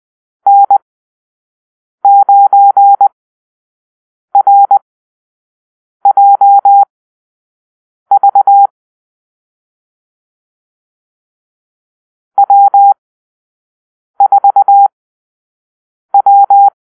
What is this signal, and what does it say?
Morse code, heard as a keyed tone, about 20 wpm, spaced as for 6 wpm, 800 Hz. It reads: N9RJV W4W
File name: arrl / Call-Sign Drill-5